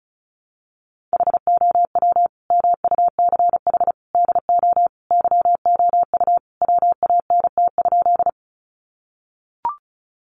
Morse code, heard as a keyed tone, about 34 wpm, 700 Hz. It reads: HOW MUCH DO YOU WANT?
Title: Sentences from Top 200 Words 35wpm